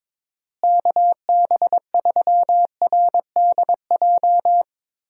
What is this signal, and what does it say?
Morse code, heard as a keyed tone, about 22 wpm, 700 Hz. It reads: KB3RDJ
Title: USA Call Signs - 6 Characters - Repeated 2x - 22wpm